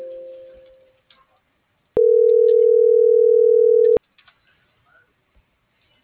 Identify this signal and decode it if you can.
Ringback tone (North America), 440 + 480 Hz.